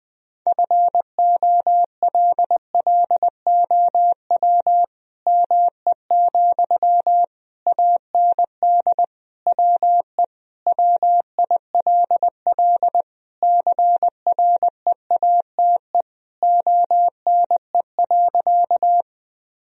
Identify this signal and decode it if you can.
Morse code, heard as a keyed tone, about 21 wpm, 700 Hz. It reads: FOLLOW ME, AND WE WILL CREATE ONE.
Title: Sentences from Top 1000 Words - Review Part 3 - 20wpm